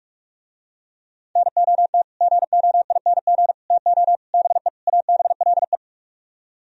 Morse code, heard as a keyed tone, about 44 wpm, 700 Hz. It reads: NOT GOING TO BE ABLE